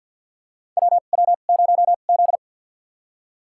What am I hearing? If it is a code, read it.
Morse code, heard as a keyed tone, about 49 wpm, spaced as for 36 wpm, 700 Hz. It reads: WW0Z
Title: USA Call Signs - 4 Characters - 2x Character Spacing - 50wpm